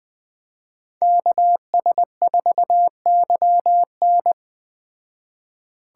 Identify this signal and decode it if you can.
Morse code, heard as a keyed tone, about 21 wpm, 700 Hz. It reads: KS4YN
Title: Speed Racing - USA 2x2 Call Signs 17wpm